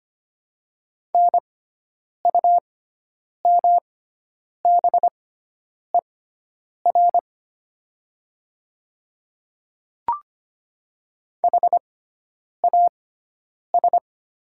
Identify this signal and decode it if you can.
Morse code, heard as a keyed tone, about 25 wpm, spaced as for 9 wpm, 690 Hz. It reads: NUMBER HAS
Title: 6x Character Spacing - Top 100 Words - 25wpm